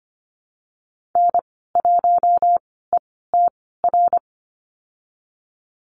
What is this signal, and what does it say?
Morse code, heard as a keyed tone, about 25 wpm, spaced as for 16 wpm, 700 Hz. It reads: N1ETR
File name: Lesson 15g - Number 5 - Call Signs 25wpm